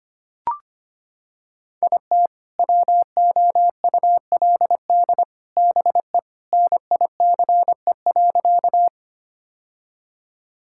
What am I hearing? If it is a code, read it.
Morse code, heard as a keyed tone, about 25 wpm, 700 Hz. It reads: IT WOULD BE NICE.
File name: practice / 25wpm / Sentences with 3-Word Phrases - 25wpm